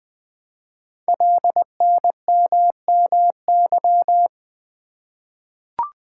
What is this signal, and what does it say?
Morse code, heard as a keyed tone, about 20 wpm, 700 Hz. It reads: LNMMY
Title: Code Groups - 5 Letters - 20wpm